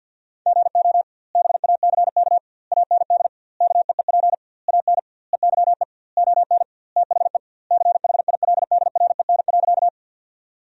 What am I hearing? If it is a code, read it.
Morse code, heard as a keyed tone, about 50 wpm, 700 Hz. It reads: GO BACK AND KEEP AN EYE ON THE CHILDREN.
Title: 2x Word-Spacing - Sentences from Top 300 Words - 50wpm